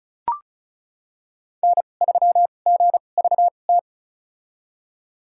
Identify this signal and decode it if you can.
Morse code, heard as a keyed tone, about 34 wpm, spaced as for 25 wpm, 700 Hz. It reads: N3GVT
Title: USA Call Signs - 5 Characters - 2x Character Spacing - 35wpm